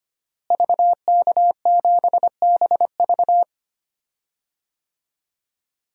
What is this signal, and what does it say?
Morse code, heard as a keyed tone, about 25 wpm, 700 Hz. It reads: VK7BV
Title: DXCC-100 Call Signs - 5 Characters - Repeated 2x - 25wpm